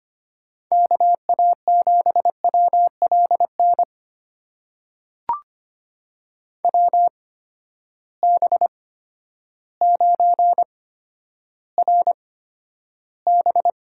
Morse code, heard as a keyed tone, about 25 wpm, spaced as for 11 wpm, 700 Hz. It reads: KA7WLN WB9RB
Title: USA Call Signs - 6 Characters - 8x Character Spacing - 25wpm